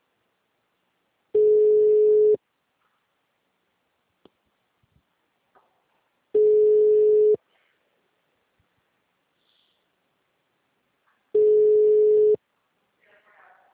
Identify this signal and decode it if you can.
Ringback tone (Europe, 425 Hz), with speech.
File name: out- -2002- - - .92